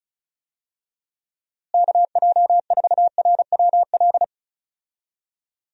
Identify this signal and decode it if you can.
Morse code, heard as a keyed tone, about 34 wpm, 700 Hz. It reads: KJ4RWL
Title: USA Call Signs - 6 Characters - Repeated 3x - 35wpm